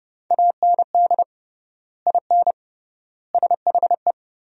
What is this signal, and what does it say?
Morse code, heard as a keyed tone, about 30 wpm, 700 Hz. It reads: AND IN SHE
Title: 3x Word-Spacing - Sentences from Top 200 Words - 30wpm